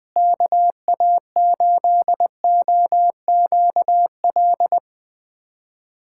Morse code, heard as a keyed tone, about 20 wpm, 700 Hz. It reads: KA8OQL
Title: USA Callsign DE Callsign 20wpm